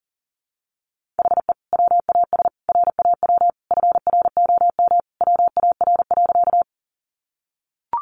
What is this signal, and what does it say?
Morse code, heard as a keyed tone, about 40 wpm, 700 Hz. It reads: HE WAS RAW FROM WAR.
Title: Sentences from Top 1500 Words 40wpm